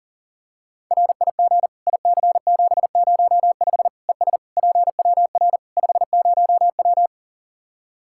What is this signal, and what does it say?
Morse code, heard as a keyed tone, about 40 wpm, 700 Hz. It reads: RIG IC705 ES PWR 50W
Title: K1USN SST QSO Practice 40wpm